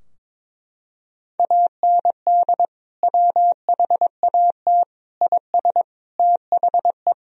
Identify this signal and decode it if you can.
Morse code, heard as a keyed tone, about 22 wpm, 700 Hz. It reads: AND WHAT IS THE
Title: Sentences from Top 1000 Words - Review Part 3 - 22wpm